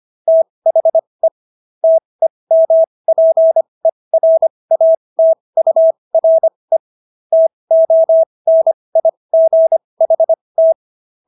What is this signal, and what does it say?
Morse code, heard as a keyed tone, about 25 wpm, spaced as for 20 wpm, 650 Hz. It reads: THE TEMPERATURE TONIGHT